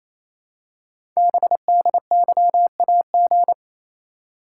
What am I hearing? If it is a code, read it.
Morse code, heard as a keyed tone, about 27 wpm, 700 Hz. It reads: BDYAG
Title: Code Groups - 5 Letters - 28wpm